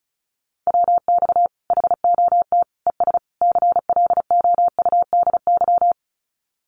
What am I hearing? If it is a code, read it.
Morse code, heard as a keyed tone, about 34 wpm, 700 Hz. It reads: WX HOT ES CLOUDY
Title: Common QSO Phrases 35wpm